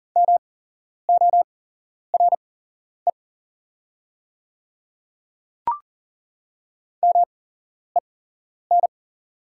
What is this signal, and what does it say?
Morse code, heard as a keyed tone, about 40 wpm, spaced as for 11 wpm, 700 Hz. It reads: MORE MEN
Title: 8x Character Spacing - Top 300 Words - 40wpm